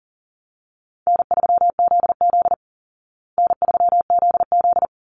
Morse code, heard as a keyed tone, about 37 wpm, 700 Hz. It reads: N3ZZ N3ZZ
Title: Easy Call Signs Repeated 2x - 40wpm